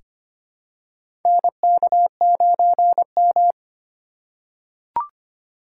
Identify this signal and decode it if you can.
Morse code, heard as a keyed tone, about 25 wpm, 700 Hz. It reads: NK9M
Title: USA Call Signs - 8x Character Spacing - 25wpm